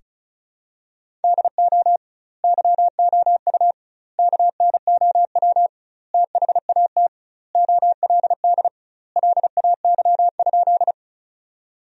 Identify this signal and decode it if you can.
Morse code, heard as a keyed tone, about 34 wpm, 700 Hz. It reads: DO YOU KNOW THAT OLD LAY?
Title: 2x Word-Spacing - Sentences from Top 400 Words - 35wpm